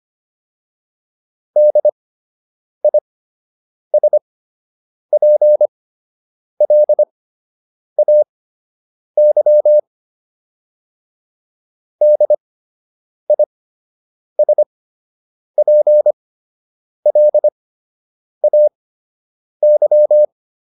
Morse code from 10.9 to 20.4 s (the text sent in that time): DISPLAY